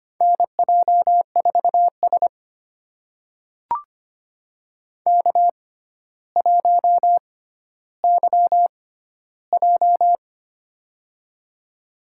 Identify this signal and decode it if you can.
Morse code, heard as a keyed tone, about 25 wpm, spaced as for 13 wpm, 700 Hz. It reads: NJ4S K1YJ
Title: USA Call Signs - 4 Characters - 6x Character Spacing - 25wpm